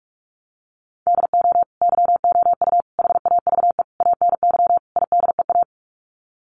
Morse code, heard as a keyed tone, about 44 wpm, 700 Hz. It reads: DO YOU HAVE ANY IDEA
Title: Sets of 5 Words Encore 45wpm